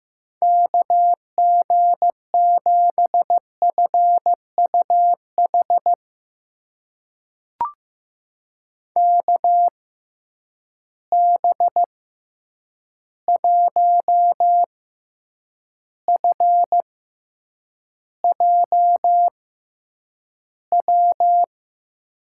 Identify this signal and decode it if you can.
Morse code, heard as a keyed tone, about 15 wpm, spaced as for 8 wpm, 700 Hz. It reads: KG7FUH KB1FJW
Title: USA Call Signs - 6 Characters - 6x Character Spacing - 15wpm